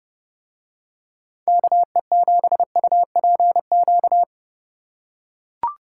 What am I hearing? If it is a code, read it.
Morse code, heard as a keyed tone, about 28 wpm, 700 Hz. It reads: KE7UPQ